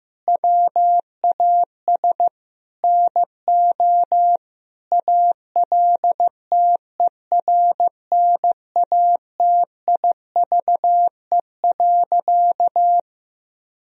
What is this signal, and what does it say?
Morse code, heard as a keyed tone, about 15 wpm, 700 Hz. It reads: WAS NO ALTERNATIVE.
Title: Sentences from Top 1000 Words - Review Part 4 - 15wpm